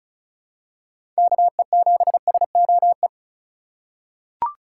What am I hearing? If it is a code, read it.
Morse code, heard as a keyed tone, about 34 wpm, 700 Hz. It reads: KE7SOE
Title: USA Call Signs - 2x Character Spacing - 35wpm